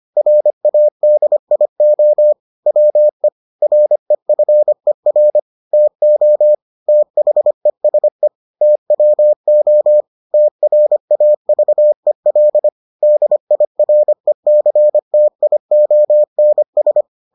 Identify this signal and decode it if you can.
Morse code, heard as a keyed tone, about 25 wpm, 600 Hz. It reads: RADIO WE REFER TO THESE TWO TRAVEL DIRECTIONS